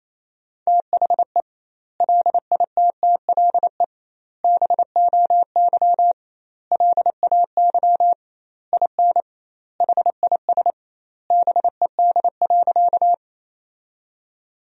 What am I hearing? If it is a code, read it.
Morse code, heard as a keyed tone, about 28 wpm, 700 Hz. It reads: THE LITTLE BOY LAY IN HIS BED.
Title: 2x Word-Spacing - Sentences from Top 500 Words - 28wpm